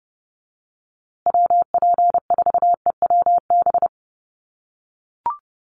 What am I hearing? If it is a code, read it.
Morse code, heard as a keyed tone, about 30 wpm, 700 Hz. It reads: WP4EWB
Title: USA Call Signs Repeated 2x 30wpm